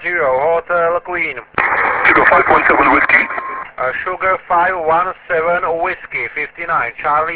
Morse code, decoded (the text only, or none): none